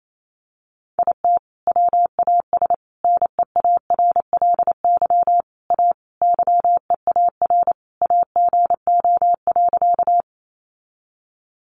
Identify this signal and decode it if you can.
Morse code, heard as a keyed tone, about 28 wpm, 700 Hz. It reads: IT WAS NEARLY A YEAR AGO.
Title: Sentences from Top 1300 Words 28wpm